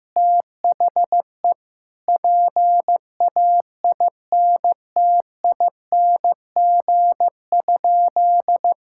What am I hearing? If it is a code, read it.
Morse code, heard as a keyed tone, about 15 wpm, 700 Hz. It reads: THE PAINTING?
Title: Sentences from Top 1000 Words - Review Part 6 - 15wpm